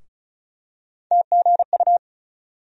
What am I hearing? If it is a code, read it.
Morse code, heard as a keyed tone, about 34 wpm, 700 Hz. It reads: TGU
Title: USA Call Sign Postfixes - 3 Letters 35wpm